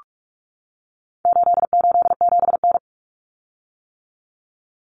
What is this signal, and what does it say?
Morse code, heard as a keyed tone, about 44 wpm, 700 Hz. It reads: 887N
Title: Common Cut Numbers - 4-Digit 45wpm